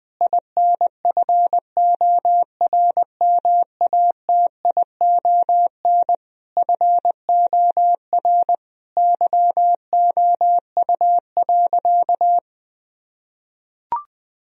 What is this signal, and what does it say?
Morse code, heard as a keyed tone, about 20 wpm, 700 Hz. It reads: INFORMATION FOR YOU.
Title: Sentences from Top 1000 Words - Review Part 3 - 20wpm